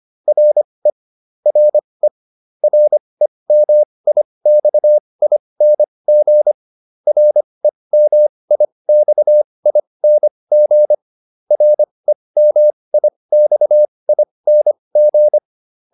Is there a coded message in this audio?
Yes: Morse code from 0.3 to 15.4 s, heard as a keyed tone, about 25 wpm, spaced as for 20 wpm, 600 Hz. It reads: RE RE REMIXING REMIXING REMIXING